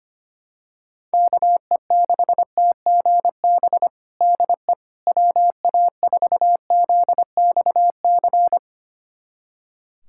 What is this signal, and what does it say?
Morse code, heard as a keyed tone, about 25 wpm, 700 Hz. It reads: KE6TGB DE WA4ZXC
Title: USA Callsign DE Callsign 25wpm